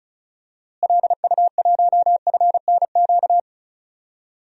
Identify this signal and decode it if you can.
Morse code, heard as a keyed tone, about 36 wpm, 700 Hz. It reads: LU1FNQ
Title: DXCC-100 Call Signs - 6 Characters - Repeated 2x - 35wpm